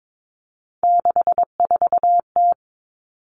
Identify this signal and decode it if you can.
Morse code, heard as a keyed tone, about 22 wpm, 700 Hz. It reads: 64T
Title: Common Cut Numbers - 3-Digit 22wpm